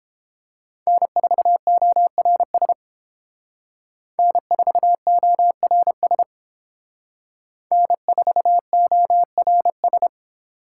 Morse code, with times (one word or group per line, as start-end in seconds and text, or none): 0.9-2.7 N4ORS
4.2-6.2 N4ORS
7.7-10.1 N4ORS